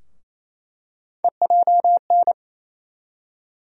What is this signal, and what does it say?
Morse code, heard as a keyed tone, about 28 wpm, 700 Hz. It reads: EJN
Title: USA Call Sign Postfixes - 3 Letters 28wpm